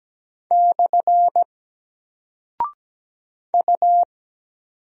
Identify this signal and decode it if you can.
Morse code, heard as a keyed tone, about 17 wpm, 700 Hz. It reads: / U
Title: Single Character (Letter:Number:Punctuation:Prosign) 17wpm